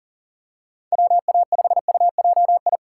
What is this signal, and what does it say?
Morse code, heard as a keyed tone, about 37 wpm, 700 Hz. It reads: WA5UJI
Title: USA Call Signs - 3x Character Spacing - 40wpm